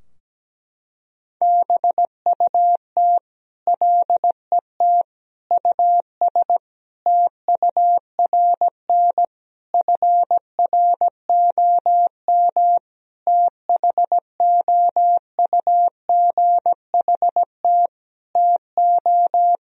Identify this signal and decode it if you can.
Morse code, heard as a keyed tone, about 17 wpm, 700 Hz. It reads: BUT LET US TURN FROM THOUGHT TO
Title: Sentences from Top 1000 Words - Review Part 4 - 17wpm